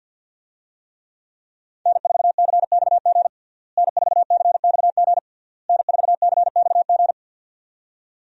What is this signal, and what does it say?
Morse code, heard as a keyed tone, about 51 wpm, 700 Hz. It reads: N4CXG N4CXG N4CXG
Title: USA Call Signs - 5 Characters - Repeated 3x - 50wpm